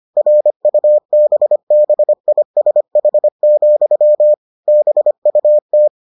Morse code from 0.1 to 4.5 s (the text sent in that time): RUBBISH,